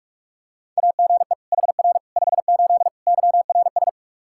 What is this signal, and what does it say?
Morse code, heard as a keyed tone, about 42 wpm, 700 Hz. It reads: AGE HR 58 YRS
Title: Practice QSO - Exchange 3 - More Info - 45wpm